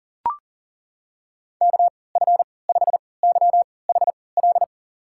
Morse code, heard as a keyed tone, about 40 wpm, spaced as for 23 wpm, 700 Hz. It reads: KF5YHL
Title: USA Call Signs - 6 Characters - 3x Character Spacing - 40wpm